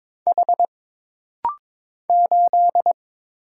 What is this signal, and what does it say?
Morse code, heard as a keyed tone, about 22 wpm, 700 Hz. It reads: H 8
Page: This rapid-fire timing is geared to push your ICR (Instant Character Recognition) skills.